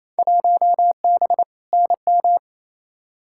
Morse code, heard as a keyed tone, about 28 wpm, 700 Hz. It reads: 1B NM
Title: Field Day Exchanges Encore - 28wpm